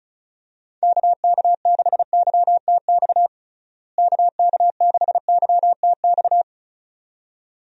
Morse code, heard as a keyed tone, about 36 wpm, 700 Hz. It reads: KK6YTX KK6YTX
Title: USA Call Signs - 6 Characters - Repeated 2x - 35wpm